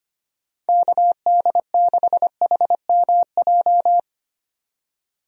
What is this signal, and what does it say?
Morse code, heard as a keyed tone, about 25 wpm, 700 Hz. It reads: KD6HMJ
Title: USA Call Signs - 6 Characters - 25wpm